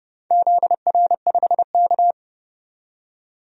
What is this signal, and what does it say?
Morse code, heard as a keyed tone, about 30 wpm, 700 Hz. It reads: ZR5K
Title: DXCC-100 Call Signs - 4 Characters - 30wpm